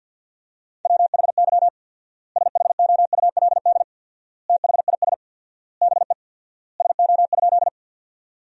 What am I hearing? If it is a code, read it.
Morse code, heard as a keyed tone, about 52 wpm, 700 Hz. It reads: WHY SHOULD THIS BE SO?
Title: 4x Word-Spacing - Sentences from Top 300 Words - 50wpm